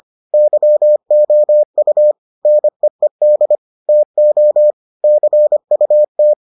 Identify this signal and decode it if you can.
Morse code, heard as a keyed tone, about 25 wpm, 600 Hz. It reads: YOU NEED TO CUT